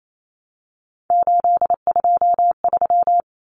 Morse code, heard as a keyed tone, about 28 wpm, 700 Hz. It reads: 823
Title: Common Cut Numbers - 3-Digit 28wpm